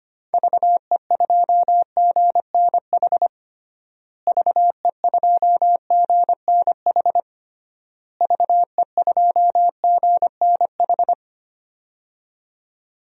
Morse code, heard as a keyed tone, about 25 wpm, 700 Hz. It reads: VE2GNH VE2GNH VE2GNH
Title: DXCC-100 Call Signs - 6 Characters - Repeated 3x - 25wpm